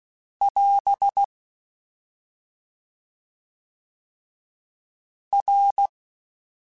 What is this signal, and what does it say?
Morse code, heard as a keyed tone, about 16 wpm, 800 Hz. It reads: <AS> R